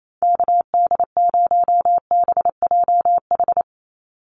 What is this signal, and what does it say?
Morse code, heard as a keyed tone, about 28 wpm, 700 Hz. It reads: KD0BJH
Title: USA Call Signs 28wpm